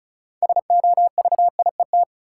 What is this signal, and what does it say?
Morse code, heard as a keyed tone, about 36 wpm, 700 Hz. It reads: SOVIET
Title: Sets of 5 Words Encore - New Word Drill - 35wpm